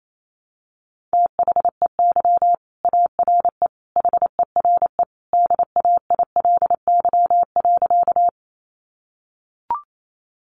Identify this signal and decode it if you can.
Morse code, heard as a keyed tone, about 28 wpm, 700 Hz. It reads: THEY ARE HERE DAILY.
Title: Sentences from Top 1100 Words 28wpm